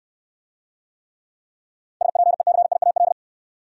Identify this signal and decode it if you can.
Morse code, heard as a keyed tone, about 69 wpm, 700 Hz. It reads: SPECIAL